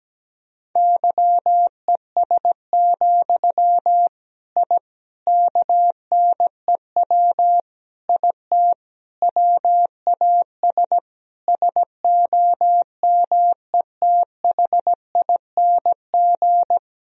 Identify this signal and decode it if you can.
Morse code, heard as a keyed tone, about 17 wpm, 700 Hz. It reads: YES, I KNEW IT WAS SOMETHING
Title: Sentences from Top 1000 Words - Review Part 2 - 17wpm